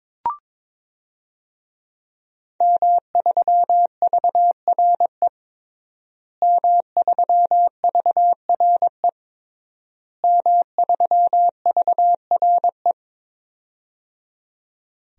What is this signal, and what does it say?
Morse code, heard as a keyed tone, about 21 wpm, 700 Hz. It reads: M3VRE M3VRE M3VRE
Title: DXCC-100 Call Signs - 5 Characters - Repeated 3x - 22wpm